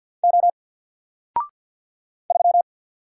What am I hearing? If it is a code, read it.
Morse code, heard as a keyed tone, about 48 wpm, 700 Hz. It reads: O 3
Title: Single Letter-Number Alt 50wpm